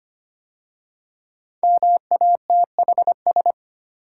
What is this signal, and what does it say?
Morse code, heard as a keyed tone, about 25 wpm, 700 Hz. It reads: MATHS
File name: 5 Letter Words - First 3 Letters Given - 25wpm